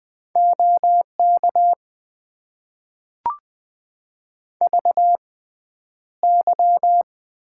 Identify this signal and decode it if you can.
Morse code, heard as a keyed tone, about 20 wpm, spaced as for 10 wpm, 700 Hz. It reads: OK VY
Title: Top 90 QSO Elements - 6x Character Spacing - 20wpm